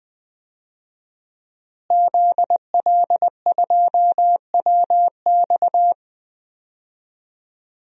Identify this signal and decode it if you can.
Morse code, heard as a keyed tone, about 20 wpm, 700 Hz. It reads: ZL2WX
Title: DXCC-100 Call Signs - 5 Characters - Repeated 2x - 20wpm